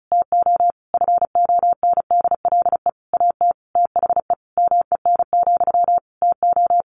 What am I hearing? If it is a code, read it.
Morse code, heard as a keyed tone, about 34 wpm, 700 Hz. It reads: TO FONDLE AT THE MEN, TO
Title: A Sound Of Thunder 35wpm